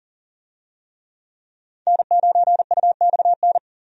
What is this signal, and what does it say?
Morse code, heard as a keyed tone, about 37 wpm, 700 Hz. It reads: N9UXN
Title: USA Call Signs - 5 Characters - 40wpm